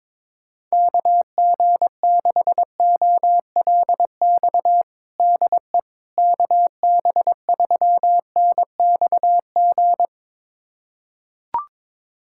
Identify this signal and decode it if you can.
Morse code, heard as a keyed tone, about 22 wpm, 700 Hz. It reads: KG6OLX DE KB3NXG
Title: USA Callsign DE Callsign - 6 Characters - 22wpm